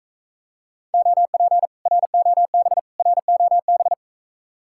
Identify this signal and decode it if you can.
Morse code, heard as a keyed tone, about 45 wpm, 700 Hz. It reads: OP ROB ROB